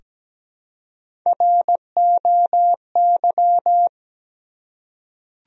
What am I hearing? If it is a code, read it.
Morse code, heard as a keyed tone, about 17 wpm, 700 Hz. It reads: ROY
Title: USA Common Names - Max 3 Letters - 17wpm